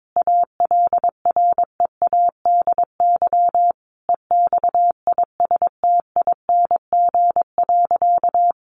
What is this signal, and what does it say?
Morse code, heard as a keyed tone, about 22 wpm, 700 Hz. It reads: ALREADY EXISTING.
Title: Sentences from Top 1500 Words 22wpm